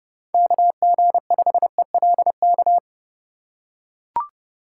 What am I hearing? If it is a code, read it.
Morse code, heard as a keyed tone, about 32 wpm, 700 Hz. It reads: KG5ELK
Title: USA Call Signs - 4x Character Spacing - 30wpm